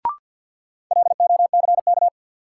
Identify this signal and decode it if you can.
Morse code, heard as a keyed tone, about 53 wpm, 700 Hz. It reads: LOCK
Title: Sentences from Top 1300 Words - New Word Drill Alt - 50wpm